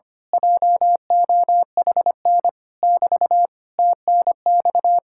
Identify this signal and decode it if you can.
Morse code, heard as a keyed tone, about 24 wpm, 700 Hz. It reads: JOHN <BT> TNX
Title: Speed-Racing Common QSO Phrases Encore - 25wpm